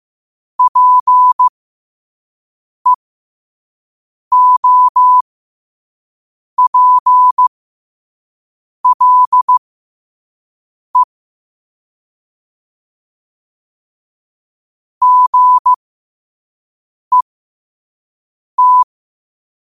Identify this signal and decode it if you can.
Morse code, heard as a keyed tone, about 15 wpm, spaced as for 5 wpm, 1000 Hz. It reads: PEOPLE GET
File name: qotdfiles / 5wpm